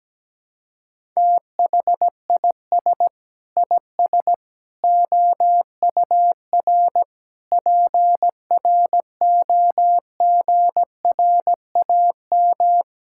Morse code, heard as a keyed tone, about 17 wpm, 700 Hz. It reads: THIS IS OUR PROGRAM
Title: Sentences from Top 1000 Words - Review Part 5 - 17wpm